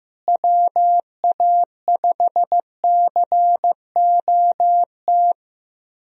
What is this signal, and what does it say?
Morse code, heard as a keyed tone, about 15 wpm, 700 Hz. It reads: WA5COT
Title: USA Call Signs - 6 Characters - Repeated 3x - 15wpm